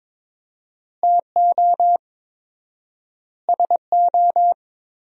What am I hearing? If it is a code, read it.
Morse code, heard as a keyed tone, about 22 wpm, 700 Hz. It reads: TO SO
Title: 4x Word-Spacing - Sentences from Top 400 Words - 22wpm